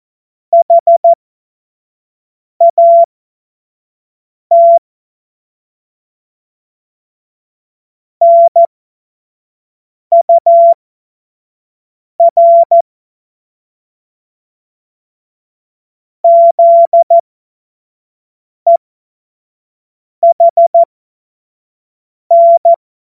Morse code, heard as a keyed tone, about 14 wpm, spaced as for 5 wpm, 680 Hz. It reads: HAT NUR ZEHN
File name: Morsekurs / Witze / Witz 25. Mai 2021, 25 F70